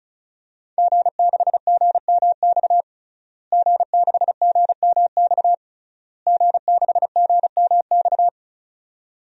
Morse code, heard as a keyed tone, about 36 wpm, 700 Hz. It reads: G6GMX G6GMX G6GMX
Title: DXCC-100 Call Signs - 5 Characters - Repeated 3x - 35wpm